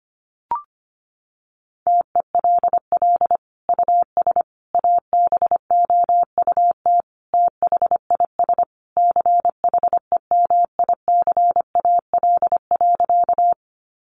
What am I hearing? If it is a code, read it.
Morse code, heard as a keyed tone, about 25 wpm, 700 Hz. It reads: TELL US ABOUT THIS CHEMICAL.
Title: Sentences from Top 1200 Words 25wpm